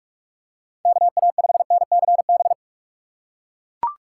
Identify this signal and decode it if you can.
Morse code, heard as a keyed tone, about 42 wpm, 700 Hz. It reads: KA5NCB
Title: USA Call Signs - 6 Characters - 2x Character Spacing - 45wpm